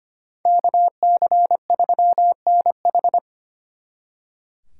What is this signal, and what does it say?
Morse code, heard as a keyed tone, about 33 wpm, 700 Hz. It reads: KC3NH